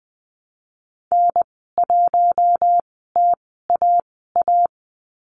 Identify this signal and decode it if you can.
Morse code, heard as a keyed tone, about 20 wpm, spaced as for 14 wpm, 700 Hz. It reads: N1TAA
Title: 042 Lesson 8g - Number 1 - Call Signs 20wpm